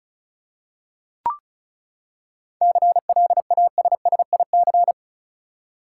Morse code, heard as a keyed tone, about 34 wpm, 700 Hz. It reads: CLASSIC
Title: Seven Letter Words - 35wpm